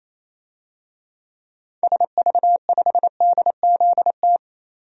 Morse code, heard as a keyed tone, about 28 wpm, 700 Hz. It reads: SV5DZT
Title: DXCC-100 Call Signs - 6 Characters - 28wpm